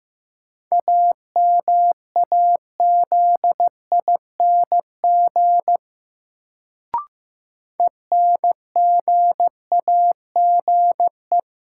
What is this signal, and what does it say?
Morse code, heard as a keyed tone, about 15 wpm, 700 Hz. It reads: AMAZING ENGAGE
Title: Sentences from Top 1500 Words - New Word Drill - 15wpm